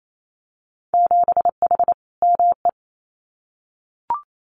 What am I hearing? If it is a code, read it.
Morse code, heard as a keyed tone, about 28 wpm, 700 Hz. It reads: 7H ME
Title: Winter Field Day Exchanges Encore 28wpm